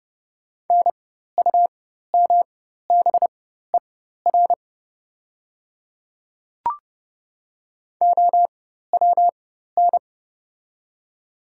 Morse code, heard as a keyed tone, about 30 wpm, spaced as for 14 wpm, 700 Hz. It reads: NUMBER OWN
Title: 4x Character Spacing - Top 500 Words - 30wpm